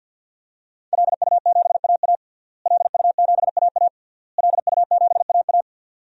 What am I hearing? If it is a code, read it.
Morse code, heard as a keyed tone, about 51 wpm, 700 Hz. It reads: LU7AA LU7AA LU7AA
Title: DXCC-100 Call Signs - 5 Characters - Repeated 3x - 50wpm